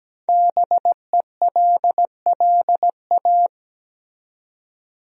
Morse code, heard as a keyed tone, about 17 wpm, 700 Hz. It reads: BELLA
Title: USA Common Names - Max 5 Letters - 17wpm